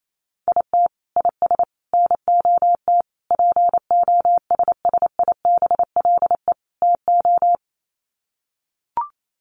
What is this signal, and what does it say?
Morse code, heard as a keyed tone, about 28 wpm, 700 Hz. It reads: IT IS NOT POSSIBLE TO
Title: Sets of 5 Words Encore2 28wpm